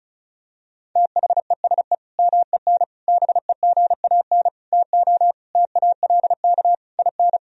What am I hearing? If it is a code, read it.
Morse code, heard as a keyed tone, about 36 wpm, 700 Hz. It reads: THESE MEN BEGAN TO TALK IN
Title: Sentences from Top 1000 Words - Review Part 1 - 35wpm